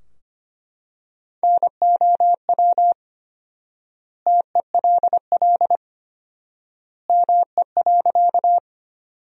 Morse code, heard as a keyed tone, about 25 wpm, 700 Hz. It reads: NOW TELL ME.
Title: 4x Word-Spacing - Sentences from Top 200 Words - 25wpm